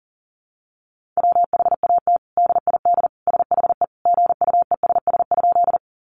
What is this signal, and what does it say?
Morse code, heard as a keyed tone, about 40 wpm, 700 Hz. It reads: WHAT DID SHE GUESS?
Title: Sentences from Top 1500 Words 40wpm